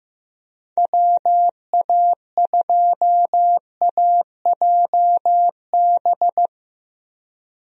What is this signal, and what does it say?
Morse code, heard as a keyed tone, about 15 wpm, 700 Hz. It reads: WA2AJB
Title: USA Call Signs - 6 Characters - Repeated 3x - 15wpm